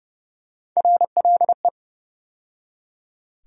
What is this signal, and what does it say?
Morse code, heard as a keyed tone, about 30 wpm, 700 Hz. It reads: RLE
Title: USA Call Sign Postfixes - 3 Letters 30wpm